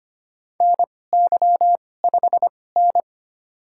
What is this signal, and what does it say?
Morse code, heard as a keyed tone, about 25 wpm, spaced as for 18 wpm, 700 Hz. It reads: NY5N
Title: USA Call Signs - 2x Character Spacing - 25wpm